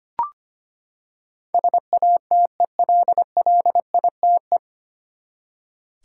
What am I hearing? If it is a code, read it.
Morse code, heard as a keyed tone, about 25 wpm, 700 Hz. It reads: SATELLITE
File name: Nine Letter Words - 25wpm